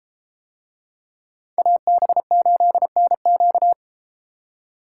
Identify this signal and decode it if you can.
Morse code, heard as a keyed tone, about 32 wpm, 700 Hz. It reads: AB8NQ